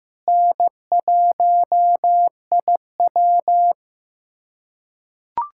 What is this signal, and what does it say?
Morse code, heard as a keyed tone, about 15 wpm, 700 Hz. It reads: N1IW
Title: USA Callsign DE Callsign - 4 Characters - 15wpm